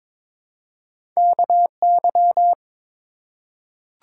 Morse code, heard as a keyed tone, about 22 wpm, 700 Hz. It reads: KY
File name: ARRL Sections v2 - 22wpm